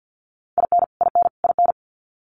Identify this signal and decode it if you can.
Morse code, heard as a keyed tone, about 49 wpm, 700 Hz. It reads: IN IN IN
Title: US States Repeated 3x 50wpm